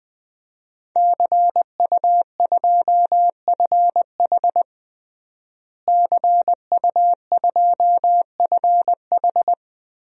Morse code, heard as a keyed tone, about 20 wpm, 700 Hz. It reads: CU2FH CU2FH
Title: DXCC-100 Call Signs - 5 Characters - Repeated 3x - 20wpm